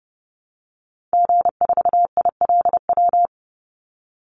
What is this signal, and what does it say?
Morse code, heard as a keyed tone, about 30 wpm, 700 Hz. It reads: G4ILW
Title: Call Signs Repeated 2x - 30wpm